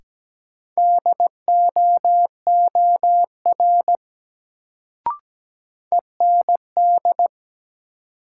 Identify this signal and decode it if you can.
Morse code, heard as a keyed tone, about 17 wpm, 700 Hz. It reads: DOOR END
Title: Sets of 2 Words - New Word Drill - 17wpm